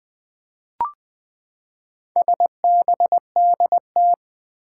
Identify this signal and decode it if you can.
Morse code, heard as a keyed tone, about 20 wpm, 700 Hz. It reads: SBDT
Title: Four Challenging Letter-Number Combinations 20wpm